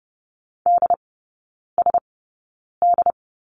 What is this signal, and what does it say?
Morse code, heard as a keyed tone, about 30 wpm, spaced as for 9 wpm, 700 Hz. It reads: DSD
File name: Lesson 17e - Letter u - Three Characters 30wpm